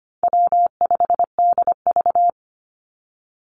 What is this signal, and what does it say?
Morse code, heard as a keyed tone, about 25 wpm, 700 Hz. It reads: W5DV
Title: Easy Call Signs Repeated 3x - 25wpm